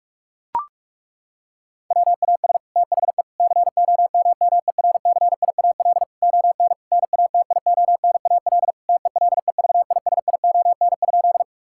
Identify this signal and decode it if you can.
Morse code, heard as a keyed tone, about 41 wpm, 700 Hz. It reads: WAS THE COMMERCIAL ON NATIONAL TELEVISION?
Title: Sentences from Top 1000 Words - Review Part 5 - 45wpm